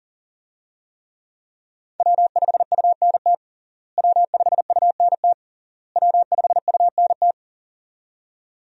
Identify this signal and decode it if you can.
Morse code, heard as a keyed tone, about 40 wpm, 700 Hz. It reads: W5UNT W5UNT W5UNT
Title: USA Call Signs - 5 Characters - Repeated 3x - 40wpm